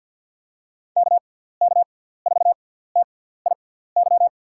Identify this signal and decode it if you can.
Morse code, heard as a keyed tone, about 46 wpm, spaced as for 17 wpm, 700 Hz. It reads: KK4TIY
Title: USA Call Signs - 6 Characters - 6x Character Spacing - 50wpm